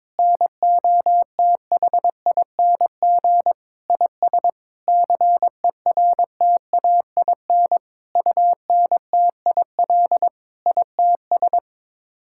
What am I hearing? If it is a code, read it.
Morse code, heard as a keyed tone, about 22 wpm, 700 Hz. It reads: NOTHING IS CERTAIN UNTIL ITS
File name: practice / 22wpm / Sets of 8 Words - 22wpm